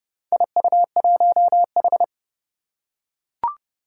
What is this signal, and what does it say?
Morse code, heard as a keyed tone, about 30 wpm, 700 Hz. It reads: IU1H
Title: Four Challenging Letter-Number Combinations 30wpm